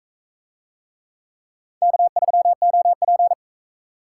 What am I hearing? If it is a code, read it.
Morse code, heard as a keyed tone, about 42 wpm, 700 Hz. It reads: K3OP